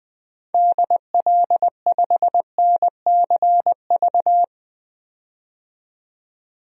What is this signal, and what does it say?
Morse code, heard as a keyed tone, about 20 wpm, 700 Hz. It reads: DL5NCV
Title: DXCC-100 Call Signs - 6 Characters - 20wpm